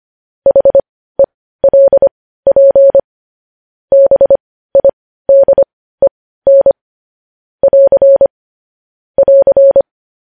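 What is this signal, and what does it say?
Morse code, heard as a keyed tone, about 25 wpm, spaced as for 15 wpm, 550 Hz. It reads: HELP BIDEN <AR> <AR>
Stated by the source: News headlines sent in Morse code at 15 wpm.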